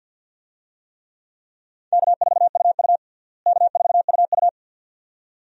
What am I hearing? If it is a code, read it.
Morse code, heard as a keyed tone, about 52 wpm, 700 Hz. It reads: K4UU K4UU
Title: USA Call Signs - 4 Characters - Repeated 2x - 50wpm